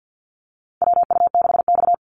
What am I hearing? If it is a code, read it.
Morse code, heard as a keyed tone, about 52 wpm, 700 Hz. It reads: WU6X
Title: Call Signs Repeated 2x - 50wpm